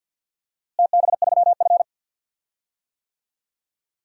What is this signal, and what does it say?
Morse code, heard as a keyed tone, about 53 wpm, 700 Hz. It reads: TB3F
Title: DXCC-100 Call Signs - 4 Characters - Repeated 2x - 50wpm